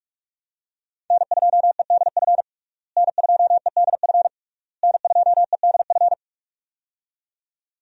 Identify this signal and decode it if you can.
Morse code, heard as a keyed tone, about 46 wpm, 700 Hz. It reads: N2EDF N2EDF N2EDF
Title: USA Call Signs - 5 Characters - Repeated 3x - 45wpm